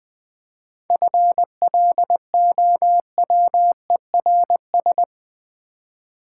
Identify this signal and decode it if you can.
Morse code, heard as a keyed tone, about 20 wpm, 700 Hz. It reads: FLOWERS
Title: Easy Words with Hints - 20wpm